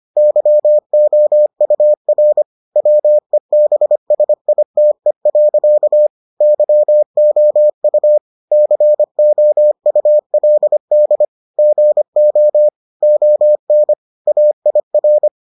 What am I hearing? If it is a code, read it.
Morse code, heard as a keyed tone, about 25 wpm, 600 Hz. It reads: YOUR WEBSITE. YOU COULD GO ON AIR